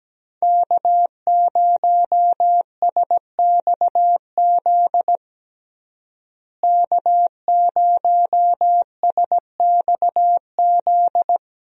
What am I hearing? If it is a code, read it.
Morse code, heard as a keyed tone, about 17 wpm, 700 Hz. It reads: K0SXZ K0SXZ
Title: USA Call Signs - 5 Characters - Repeated 3x - 17wpm